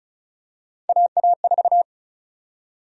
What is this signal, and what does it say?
Morse code, heard as a keyed tone, about 36 wpm, 700 Hz. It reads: AA4
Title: USA Call Sign Prefixes - 35wpm